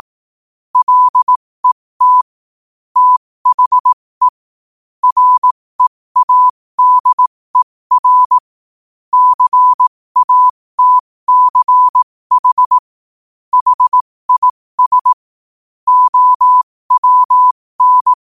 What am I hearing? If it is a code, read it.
Morse code, heard as a keyed tone, about 18 wpm, spaced as for 15 wpm, 1000 Hz. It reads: LET THE READER CATCH HIS OWN
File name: qotdfiles / 15wpm